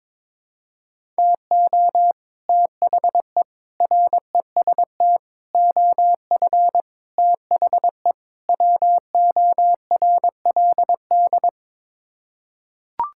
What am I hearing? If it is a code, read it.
Morse code, heard as a keyed tone, about 22 wpm, 700 Hz. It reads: TO THE REST OF THE WORLD
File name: practice / 22wpm / Sets of 6 Words - 22wpm